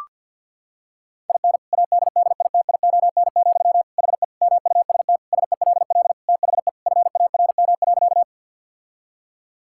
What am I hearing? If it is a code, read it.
Morse code, heard as a keyed tone, about 50 wpm, 700 Hz. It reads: IN ADDITION, HE MUST SELL THE FARM.
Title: Sentences from Top 1000 Words - Review Part 6 - 50wpm